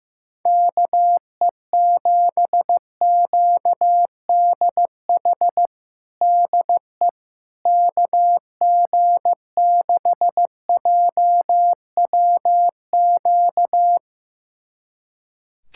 Morse code, heard as a keyed tone, about 15 wpm, 700 Hz. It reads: KE7QDH DE KG6JWQ
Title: USA Callsign DE Callsign - 6 Characters - 15wpm